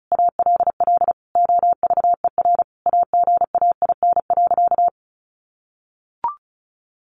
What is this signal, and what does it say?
Morse code, heard as a keyed tone, about 34 wpm, 700 Hz. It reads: ALL OVER AGAIN.
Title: Sentences from Top 500 Words 35wpm